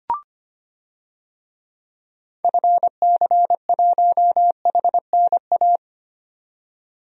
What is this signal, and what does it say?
Morse code, heard as a keyed tone, about 25 wpm, 700 Hz. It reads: FC1HNA
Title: DXCC-100 Call Signs - 6 Characters - 25wpm